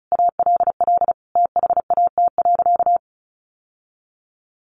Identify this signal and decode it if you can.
Morse code, heard as a keyed tone, about 34 wpm, 700 Hz. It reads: ALL THAT.
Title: Sentences from Top 1200 Words 35wpm